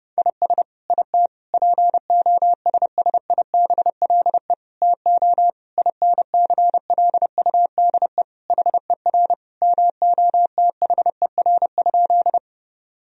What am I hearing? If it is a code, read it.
Morse code, heard as a keyed tone, about 32 wpm, 700 Hz. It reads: IS IT POSSIBLE TO INCLUDE HER MOTHER?
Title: Sentences from Top 1000 Words - Review Part 3 - 30wpm